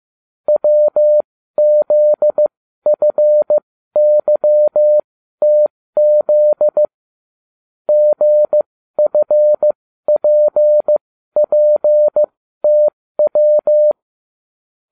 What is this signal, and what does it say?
Morse code, heard as a keyed tone, about 15 wpm, spaced as for 12 wpm, 600 Hz. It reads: WZFYTZ GFPPTW